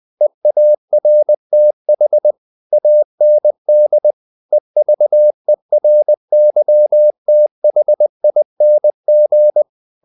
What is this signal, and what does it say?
Morse code, heard as a keyed tone, about 20 wpm, 600 Hz. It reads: EARTH AND EVERYTHING
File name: CW / Practice / If20